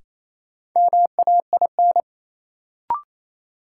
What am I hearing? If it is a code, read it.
Morse code, heard as a keyed tone, about 28 wpm, 700 Hz. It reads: MAIN
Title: Sentences from Top 1000 Words - Review Part 2 - New Word Drill - 28wpm